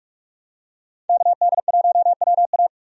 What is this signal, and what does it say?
Morse code, heard as a keyed tone, about 46 wpm, 700 Hz. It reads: KD1WA
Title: USA Call Signs - 5 Characters - 45wpm